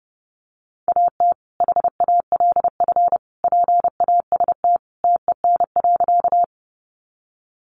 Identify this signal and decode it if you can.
Morse code, heard as a keyed tone, about 30 wpm, 700 Hz. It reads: AT HALF PAST TEN.
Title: Sentences from Top 600 Words 30wpm